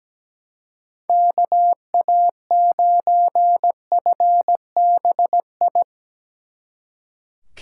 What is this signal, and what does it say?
Morse code, heard as a keyed tone, about 19 wpm, 700 Hz. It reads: KA9FBI
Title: Speed Racing - USA 2x3 Call Signs 17wpm